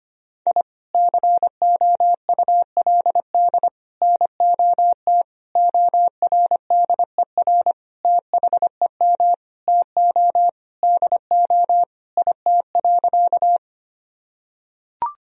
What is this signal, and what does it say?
Morse code, heard as a keyed tone, about 25 wpm, 700 Hz. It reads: I COULD NOT ORDER THEM TO DO IT.
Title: Sentences from Top 1000 Words - Review Part 2 - 25wpm